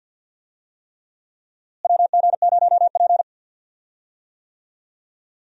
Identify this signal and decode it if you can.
Morse code, heard as a keyed tone, about 52 wpm, 700 Hz. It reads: WG0P
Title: USA Call Signs - 4 Characters - Repeated 3x - 50wpm